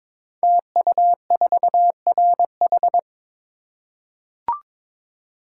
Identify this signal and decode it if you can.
Morse code, heard as a keyed tone, about 22 wpm, 700 Hz. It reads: TU4RH
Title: Code Groups - 5 Letter:Numbers - 22wpm